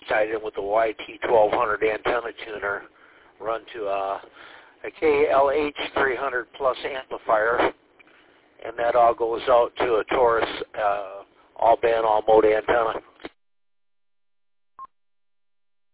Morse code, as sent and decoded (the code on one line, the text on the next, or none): none